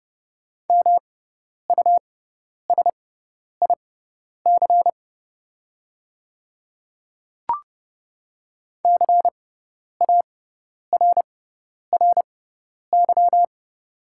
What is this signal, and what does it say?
Morse code, heard as a keyed tone, about 30 wpm, spaced as for 10 wpm, 700 Hz. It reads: MUSIC CARRY
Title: 6x Character Spacing - Top 300 Words - 30wpm